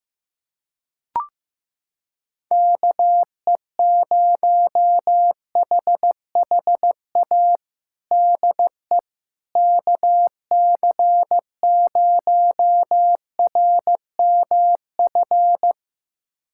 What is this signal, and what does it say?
Morse code, heard as a keyed tone, about 15 wpm, 700 Hz. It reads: KE0HHA DE KC0RMF
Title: USA Callsign DE Callsign 15wpm